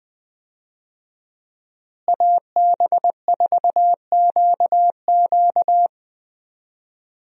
Morse code, heard as a keyed tone, about 20 wpm, 700 Hz. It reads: AB4QQ
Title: USA Call Signs - 5 Characters - Repeated 2x - 20wpm